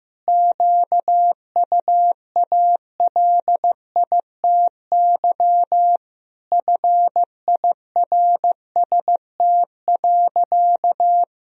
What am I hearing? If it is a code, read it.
Morse code, heard as a keyed tone, about 15 wpm, 700 Hz. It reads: QUALITY FIRST.
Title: Sentences from Top 1000 Words - Review Part 4 - 15wpm